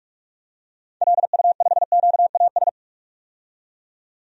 Morse code, heard as a keyed tone, about 46 wpm, 700 Hz. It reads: LU5QAS
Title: DXCC-100 Call Signs - 6 Characters - 45wpm